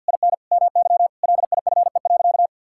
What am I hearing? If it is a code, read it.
Morse code, heard as a keyed tone, about 45 wpm, 690 Hz. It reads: IN MY LIFE.
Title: Sentences from Top 1000 Words - Review Part 5 - 50wpm